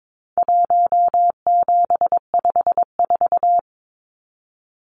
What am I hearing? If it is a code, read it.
Morse code, heard as a keyed tone, about 22 wpm, 700 Hz. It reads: 1754
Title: Numbers - Contest Style - 4-Digit 22wpm